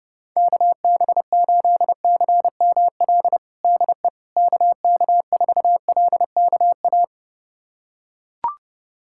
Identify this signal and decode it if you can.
Morse code, heard as a keyed tone, about 30 wpm, 700 Hz. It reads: KB8CML DE KK4LKA
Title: USA Callsign DE Callsign 30wpm